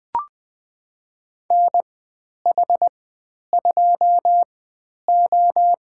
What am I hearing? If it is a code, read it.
Morse code, heard as a keyed tone, about 20 wpm, spaced as for 10 wpm, 700 Hz. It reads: NH2O
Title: 095 Lesson 16g - Letter c - Call Signs 20wpm